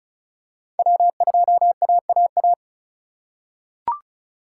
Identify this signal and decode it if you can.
Morse code, heard as a keyed tone, about 36 wpm, 700 Hz. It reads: W2AAA
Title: USA Call Signs - 4x Character Spacing - 35wpm